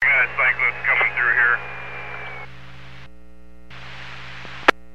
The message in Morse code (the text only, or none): none